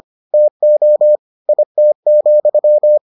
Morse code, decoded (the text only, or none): TO IT,